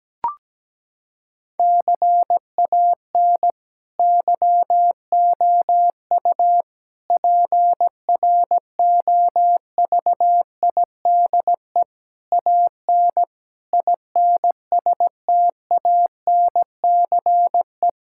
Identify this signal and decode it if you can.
Morse code, heard as a keyed tone, about 17 wpm, 700 Hz. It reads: CAN YOU PROVIDE AN INSTANCE
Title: Sentences from Top 1000 Words - Review Part 5 - 17wpm